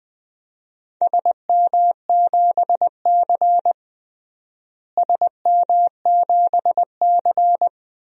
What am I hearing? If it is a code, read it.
Morse code, heard as a keyed tone, about 19 wpm, 700 Hz. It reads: SM7C SM7C
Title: DXCC-100 Call Signs - 4 Characters - Repeated 3x - 20wpm